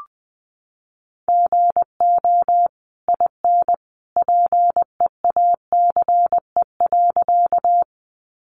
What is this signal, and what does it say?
Morse code, heard as a keyed tone, about 20 wpm, 700 Hz. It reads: GO IN PEACE.
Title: Sentences from Top 1500 Words 20wpm